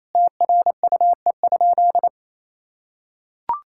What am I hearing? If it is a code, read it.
Morse code, heard as a keyed tone, about 27 wpm, 700 Hz. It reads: TRUE?
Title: Sentences from Top 1000 Words - Review Part 6 - 28wpm